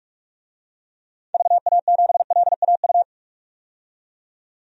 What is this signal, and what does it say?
Morse code, heard as a keyed tone, about 41 wpm, 700 Hz. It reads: VA7LAU
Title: DXCC-100 Call Signs - 6 Characters - 45wpm